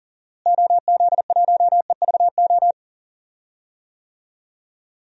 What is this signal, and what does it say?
Morse code, heard as a keyed tone, about 37 wpm, 700 Hz. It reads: OZ1EVO
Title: DXCC-100 Call Signs - 6 Characters - Repeated 2x - 40wpm